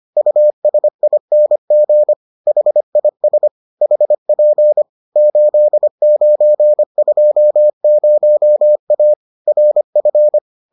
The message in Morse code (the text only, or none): USING HIS HP 8920A RF